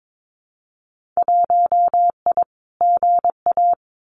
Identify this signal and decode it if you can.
Morse code, heard as a keyed tone, about 22 wpm, 700 Hz. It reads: 1I GA
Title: Winter Field Day Exchanges Encore 22wpm